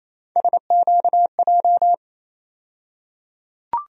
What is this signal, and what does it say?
Morse code, heard as a keyed tone, about 27 wpm, 700 Hz. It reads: SQJ
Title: USA Call Sign Postfixes - 3 Letters 28wpm